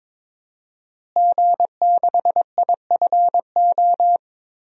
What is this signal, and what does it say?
Morse code, heard as a keyed tone, about 22 wpm, 700 Hz. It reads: G6IFO
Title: DXCC-100 Call Signs - 5 Characters - Repeated 3x - 22wpm